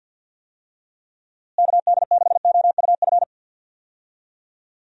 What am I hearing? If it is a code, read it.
Morse code, heard as a keyed tone, about 50 wpm, 700 Hz. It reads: KD6OUF
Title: USA Call Signs - 6 Characters - Repeated 3x - 50wpm